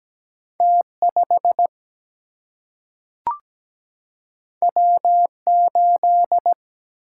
Morse code, heard as a keyed tone, about 17 wpm, 700 Hz. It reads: T5 W8
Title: Single-Letter Single-Number 17wpm